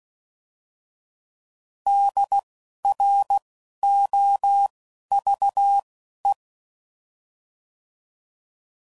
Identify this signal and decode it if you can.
Morse code, heard as a keyed tone, about 16 wpm, spaced as for 11 wpm, 800 Hz. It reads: DROVE
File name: Prefixes / By First Letter / Ds